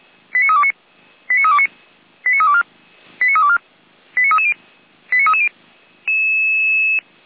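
Five-tone selective call (ZVEI), heard as quick runs of five tones, tones of about 70 ms. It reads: 89329 89329 89324 89324 89330 89330
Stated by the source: Received on an SDR as a ZVEI five-tone call.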